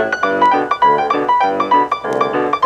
There is no code.